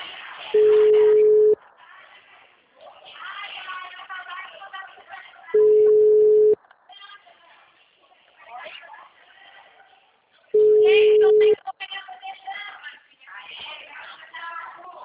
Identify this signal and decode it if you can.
Ringback tone (Europe, 425 Hz), with speech.